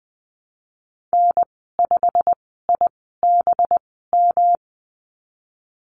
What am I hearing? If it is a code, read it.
Morse code, heard as a keyed tone, about 20 wpm, spaced as for 15 wpm, 700 Hz. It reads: N5IBM
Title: 186 Lesson 30g - Letter b - Call Signs 20wpm